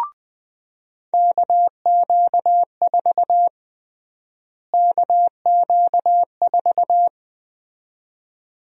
Morse code, heard as a keyed tone, about 20 wpm, 700 Hz. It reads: KQ4 KQ4
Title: USA Call Sign Prefixes - Repeated 2x - 20wpm